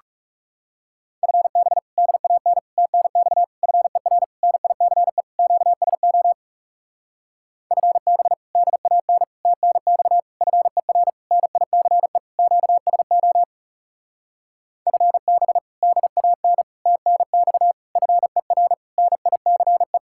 Morse code, heard as a keyed tone, about 40 wpm, 700 Hz. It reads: FB DAN TNX FER NICE QSO FB DAN TNX FER NICE QSO FB DAN TNX FER NICE